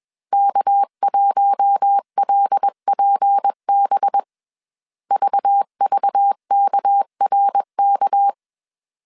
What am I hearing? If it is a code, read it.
Morse code, heard as a keyed tone, about 21 wpm, 800 Hz. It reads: K1LPB VVKRK
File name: CW / DET / 1 6 1500C S B 20wpm 2048